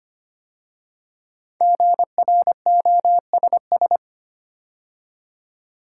Morse code, heard as a keyed tone, about 25 wpm, 700 Hz. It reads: GROSS
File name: 5 Letter Words - First 3 Letters Given - 25wpm